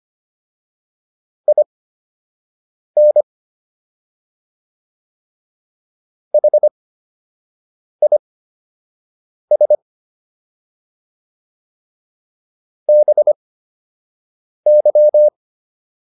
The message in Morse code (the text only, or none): IN HIS BY